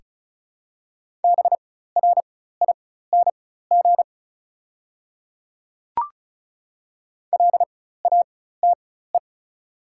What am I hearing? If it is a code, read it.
Morse code, heard as a keyed tone, about 36 wpm, spaced as for 16 wpm, 700 Hz. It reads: BRING LATE
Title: 4x Character Spacing - Top 500 Words - 35wpm